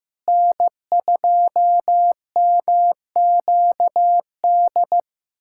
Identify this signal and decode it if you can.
Morse code, heard as a keyed tone, about 15 wpm, 700 Hz. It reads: N2MQD
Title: USA Callsign DE Callsign 15wpm